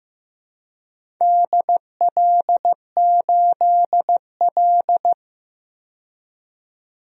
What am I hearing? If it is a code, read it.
Morse code, heard as a keyed tone, about 16 wpm, 700 Hz. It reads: DL8L
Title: DXCC-100 Call Signs - 4 Characters - 15wpm